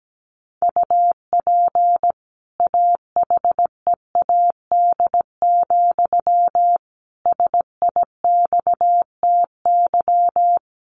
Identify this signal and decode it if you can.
Morse code, heard as a keyed tone, about 17 wpm, 700 Hz. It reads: UP AHEAD, SIXTY
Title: A Sound Of Thunder 17wpm